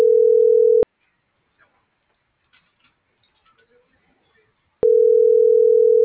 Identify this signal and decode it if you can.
Ringback tone (North America), 440 + 480 Hz.